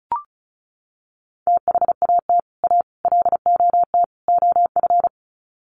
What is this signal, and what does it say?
Morse code, heard as a keyed tone, about 34 wpm, 700 Hz. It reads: THAT A LOT OF
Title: Sets of 4 Words 35wpm